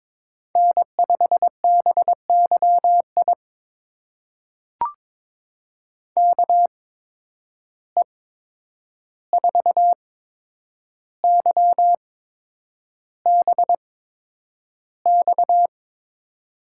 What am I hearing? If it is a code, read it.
Morse code, heard as a keyed tone, about 22 wpm, spaced as for 9 wpm, 700 Hz. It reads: N5BYI KE4YBX
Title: USA Call Signs - 8x Character Spacing - 22wpm